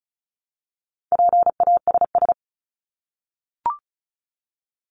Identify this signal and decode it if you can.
Morse code, heard as a keyed tone, about 34 wpm, 700 Hz. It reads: PASS
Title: ED Suffix - Top 100 Words with Base Words 35wpm